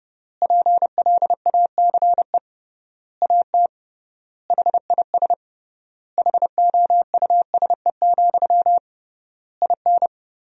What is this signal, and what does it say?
Morse code, heard as a keyed tone, about 30 wpm, 700 Hz. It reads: PLACE AT HIS HOUSE, IN
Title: 3x Word-Spacing - Sentences from Top 300 Words - 30wpm